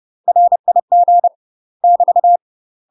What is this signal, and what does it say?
Morse code, heard as a keyed tone, about 30 wpm, 700 Hz. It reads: RIG <BT>
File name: UK MASTER QSOs TRAINING / 30WPM / UK MASTER QSOs 0032